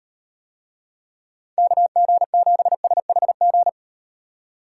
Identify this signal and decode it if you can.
Morse code, heard as a keyed tone, about 38 wpm, 700 Hz. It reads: KG7SHG